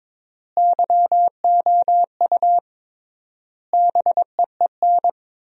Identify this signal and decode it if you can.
Morse code, heard as a keyed tone, about 22 wpm, 700 Hz. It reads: YOU BEEN
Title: 3x Word-Spacing - Sentences from Top 200 Words - 22wpm